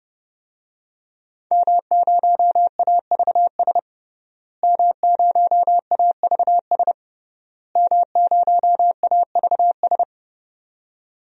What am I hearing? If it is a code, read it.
Morse code, heard as a keyed tone, about 30 wpm, 700 Hz. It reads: M0AVS M0AVS M0AVS
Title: DXCC-100 Call Signs - 5 Characters - Repeated 3x - 30wpm